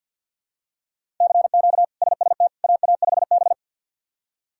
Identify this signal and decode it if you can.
Morse code, heard as a keyed tone, about 50 wpm, 700 Hz. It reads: CQ SST AA5B
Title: K1USN SST QSO Practice - 50wpm